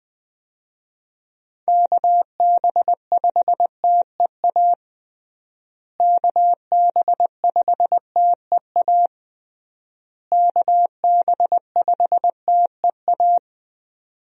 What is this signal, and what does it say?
Morse code, heard as a keyed tone, about 20 wpm, 700 Hz. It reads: KB5TEA KB5TEA KB5TEA
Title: USA Call Signs - 6 Characters - Repeated 3x - 20wpm